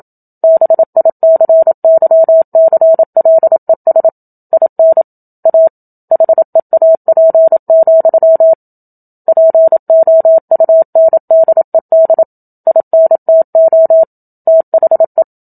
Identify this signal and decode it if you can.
Morse code, heard as a keyed tone, about 27 wpm, 650 Hz. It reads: BICYCLES IN A HEAP, POUNDED INTO THE
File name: Book Two / 01-Chapter 1 - 27 WPM